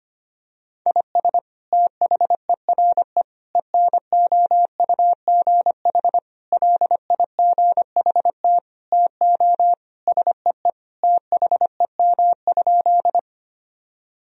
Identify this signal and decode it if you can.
Morse code, heard as a keyed tone, about 25 wpm, 700 Hz. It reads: IS THERE ENOUGH LIGHT TO SEE THEM?
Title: Sentences from Top 1000 Words - Review Part 1 - 25wpm